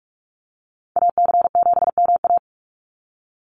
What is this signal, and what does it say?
Morse code, heard as a keyed tone, about 44 wpm, 700 Hz. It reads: AC7MA
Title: NA SOTA Activator Callsigns 45wpm